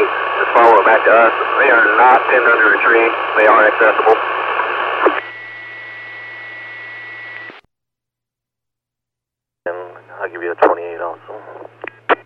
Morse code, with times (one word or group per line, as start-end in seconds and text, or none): none